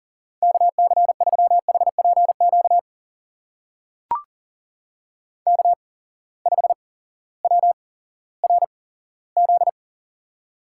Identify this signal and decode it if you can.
Morse code, heard as a keyed tone, about 38 wpm, spaced as for 18 wpm, 700 Hz. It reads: KC3HPQ K5WRZ
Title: USA Call Signs - 8x Character Spacing - 40wpm